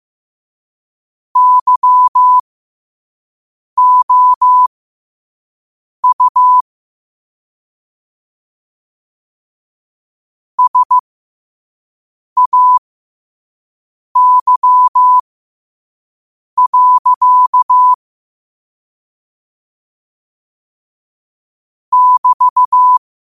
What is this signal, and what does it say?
Morse code, heard as a keyed tone, about 15 wpm, spaced as for 5 wpm, 1000 Hz. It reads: YOU SAY. <BT>
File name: qotdfiles / 5wpm